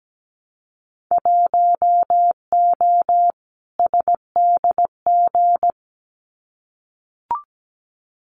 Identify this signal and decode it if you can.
Morse code, heard as a keyed tone, about 17 wpm, 700 Hz. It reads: 1O SDG
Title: Winter Field Day Exchanges Encore 17wpm